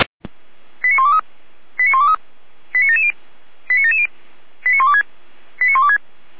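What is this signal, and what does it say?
Five-tone selective call (ZVEI), heard as quick runs of five tones, tones of about 70 ms. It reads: 89123 89123 89770 89770 89126 89126
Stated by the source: ZVEI five-tone selective call received on SDR.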